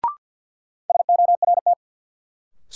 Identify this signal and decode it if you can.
Morse code, heard as a keyed tone, about 50 wpm, 700 Hz. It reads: SORT
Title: Sentences from Top 1000 Words - Review Part 3 - New Word Drill - 50wpm